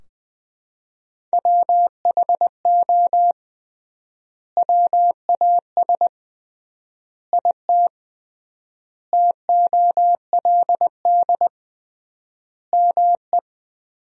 Morse code, heard as a keyed tone, about 20 wpm, 700 Hz. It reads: WHO WAS IT TOLD ME
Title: 3x Word-Spacing - Sentences from Top 400 Words - 20wpm